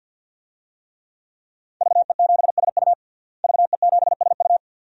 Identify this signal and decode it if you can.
Morse code, heard as a keyed tone, about 52 wpm, 700 Hz. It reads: VE7SU VE7SU
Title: DXCC-100 Call Signs - 5 Characters - Repeated 2x - 50wpm